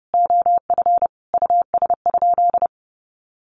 Morse code, heard as a keyed tone, about 30 wpm, 700 Hz. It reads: OF US?
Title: Sentences from Top 600 Words 30wpm